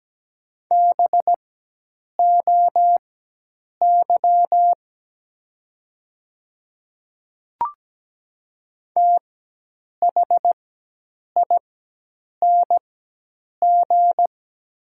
Morse code, heard as a keyed tone, about 17 wpm, spaced as for 8 wpm, 700 Hz. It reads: BOY THING
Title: 4x Character Spacing - Top 200 Words - 17wpm